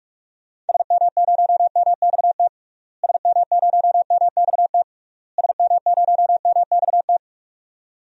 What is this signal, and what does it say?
Morse code, heard as a keyed tone, about 46 wpm, 700 Hz. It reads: SM0MXT SM0MXT SM0MXT
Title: DXCC-100 Call Signs - 6 Characters - Repeated 3x - 45wpm